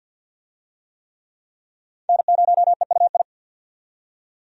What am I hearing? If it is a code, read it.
Morse code, heard as a keyed tone, about 50 wpm, 700 Hz. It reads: N0EUI
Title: USA Call Signs - 5 Characters - 50wpm